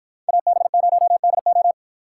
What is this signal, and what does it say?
Morse code, heard as a keyed tone, about 52 wpm, 700 Hz. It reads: AB0DO